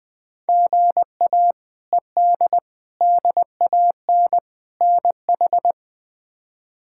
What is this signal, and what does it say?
Morse code, heard as a keyed tone, about 21 wpm, 700 Hz. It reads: GA ED DAN NH
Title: K1USN SST QSO Practice - 20wpm